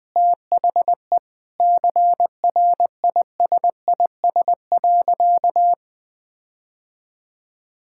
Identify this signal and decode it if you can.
Morse code, heard as a keyed tone, about 20 wpm, 700 Hz. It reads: THE CRISIS.
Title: Sentences with 8-Word Phrases - 20wpm